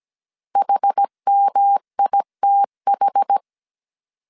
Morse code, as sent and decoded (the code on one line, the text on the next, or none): .... -- .. - ....
HMITH